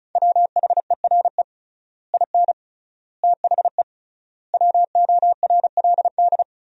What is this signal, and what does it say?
Morse code, heard as a keyed tone, about 34 wpm, 700 Hz. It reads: WHERE IN THE WORLD
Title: 3x Word-Spacing - Sentences from Top 200 Words - 35wpm